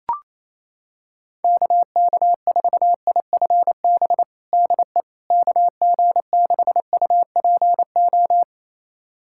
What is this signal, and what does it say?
Morse code, heard as a keyed tone, about 28 wpm, 700 Hz. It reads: KK4IFB DE KG6UPO
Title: USA Callsign DE Callsign - 6 Characters - 28wpm